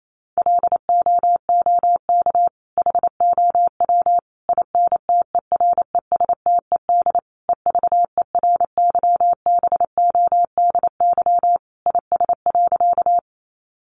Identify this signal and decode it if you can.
Morse code, heard as a keyed tone, about 28 wpm, 700 Hz. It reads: LOOK HOW INTERESTED EVERYBODY IS.
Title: Sentences from Top 1400 Words 28wpm